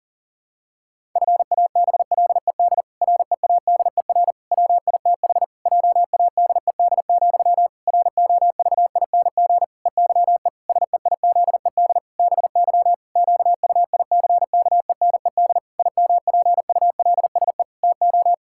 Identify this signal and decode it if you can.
Morse code, heard as a keyed tone, about 40 wpm, 700 Hz. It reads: FABLED READER WITH JADED, ROVING EYE SEIZED BY QUICKENED IMPULSE TO
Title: Pangram Challenge v2 - 40wpm